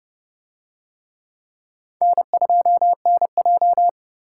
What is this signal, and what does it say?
Morse code, heard as a keyed tone, about 29 wpm, 700 Hz. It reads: N2NJ